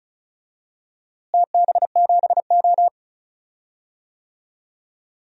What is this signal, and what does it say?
Morse code, heard as a keyed tone, about 36 wpm, 690 Hz. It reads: TB7O
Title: DXCC-100 Call Signs - 4 Characters - Repeated 2x - 35wpm